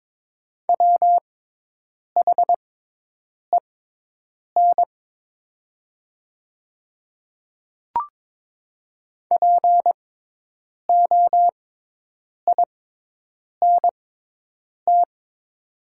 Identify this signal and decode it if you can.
Morse code, heard as a keyed tone, about 22 wpm, spaced as for 8 wpm, 700 Hz. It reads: WHEN POINT
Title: 6x Character Spacing - Top 300 Words - 22wpm